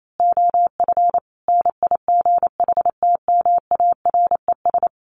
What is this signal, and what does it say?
Morse code, heard as a keyed tone, about 28 wpm, 700 Hz. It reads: OF NIGHTMARES
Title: A Sound Of Thunder 28wpm